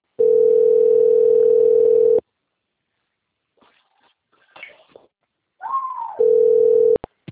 Ringback tone (North America), 440 + 480 Hz.
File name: out- - - - - .36227